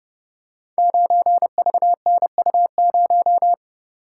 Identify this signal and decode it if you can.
Morse code, heard as a keyed tone, about 29 wpm, 700 Hz. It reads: 9VNU0
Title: Code Groups - 5 Letter:Numbers - 30wpm